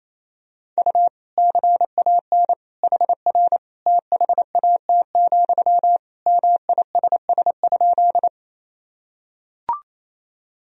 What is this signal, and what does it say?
Morse code, heard as a keyed tone, about 28 wpm, 700 Hz. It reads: U CAN HR THAT, MISS?
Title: Sentences from Top 500 Words and Abbreviations 28wpm